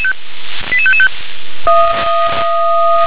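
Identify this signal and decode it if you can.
Five-tone selective call (ZVEI), heard as quick runs of five tones, tones of about 70 ms. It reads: 88555
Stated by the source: ZVEI five-tone selective call received on SDR.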